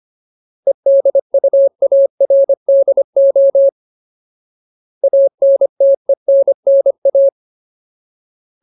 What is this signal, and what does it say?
Morse code, heard as a keyed tone, about 25 wpm, 550 Hz. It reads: EDUARDO ANTENNA
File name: Ultimate Word List / 25WPM Ultimate